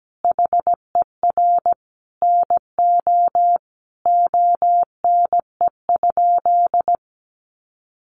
Morse code, heard as a keyed tone, about 17 wpm, 700 Hz. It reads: HER NO ONE?
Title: Sentences from Top 600 Words 17wpm